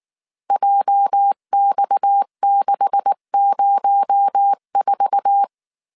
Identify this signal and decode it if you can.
Morse code, heard as a keyed tone, about 19 wpm, 800 Hz. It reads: JX604
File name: CW / DET / 1 9 1500C S B 18wpm 2551